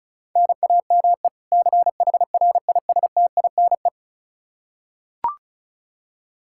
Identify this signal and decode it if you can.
Morse code, heard as a keyed tone, about 36 wpm, 700 Hz. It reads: NAME CHRISTINE
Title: Practice QSO - Exchange 1 - Beginning - 35wpm